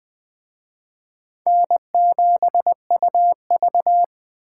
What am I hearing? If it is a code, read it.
Morse code, heard as a keyed tone, about 21 wpm, 700 Hz. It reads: N7UV